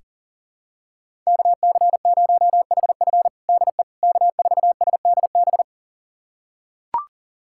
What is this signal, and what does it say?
Morse code, heard as a keyed tone, about 37 wpm, 700 Hz. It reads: KC0HF DE K4SDB
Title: USA Callsign DE Callsign - 5 Characters - 40wpm